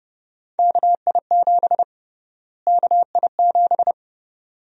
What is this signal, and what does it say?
Morse code, heard as a keyed tone, about 31 wpm, 700 Hz. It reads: KI7 KI7
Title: USA Call Sign Prefixes - Repeated 2x - 30wpm